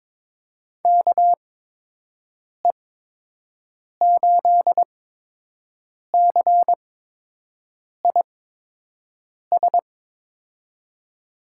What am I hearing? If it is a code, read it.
Morse code, heard as a keyed tone, about 22 wpm, spaced as for 6 wpm, 700 Hz. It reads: KE8CIS
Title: USA Call Signs - 6 Characters - 8x Character Spacing - 22wpm